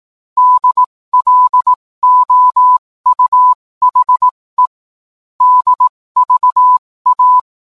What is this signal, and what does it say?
Morse code, heard as a keyed tone, about 18 wpm, spaced as for 15 wpm, 1000 Hz. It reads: DLOUHE DVA